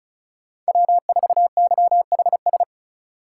Morse code, heard as a keyed tone, about 34 wpm, 700 Hz. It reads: W4YHS
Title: USA Call Signs - 2x Character Spacing - 35wpm